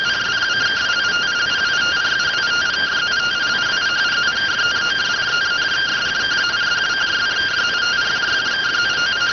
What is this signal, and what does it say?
Two-tone teleprinter signal, about 170 Hz shift.